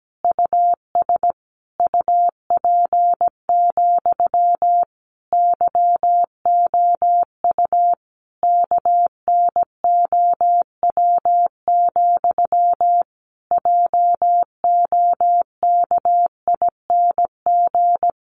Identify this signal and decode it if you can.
Morse code, heard as a keyed tone, about 17 wpm, 700 Hz. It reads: US UP, YOU KNOW, JOKING
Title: A Sound Of Thunder 17wpm